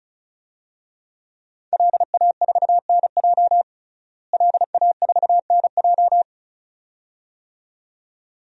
Morse code, heard as a keyed tone, about 36 wpm, 700 Hz. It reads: LA4NJ LA4NJ
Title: DXCC-100 Call Signs - 5 Characters - Repeated 2x - 35wpm